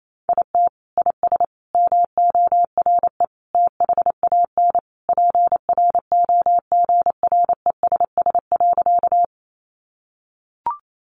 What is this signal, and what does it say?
Morse code, heard as a keyed tone, about 28 wpm, 700 Hz. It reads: IT IS MORE THAN PROGRESS.
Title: Sentences from Top 1400 Words 28wpm